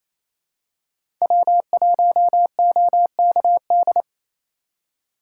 Morse code, heard as a keyed tone, about 28 wpm, 700 Hz. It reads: W1OKD
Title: USA Call Signs - 5 Characters - 6x Character Spacing - 28wpm